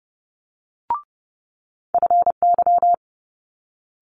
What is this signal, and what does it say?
Morse code, heard as a keyed tone, about 30 wpm, 700 Hz. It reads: FY
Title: Multi-letter Phonemes Quick Fire 30wpm